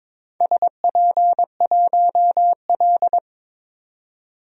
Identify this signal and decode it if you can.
Morse code, heard as a keyed tone, about 22 wpm, 690 Hz. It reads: SP1L
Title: DXCC-100 Call Signs - 4 Characters - 22wpm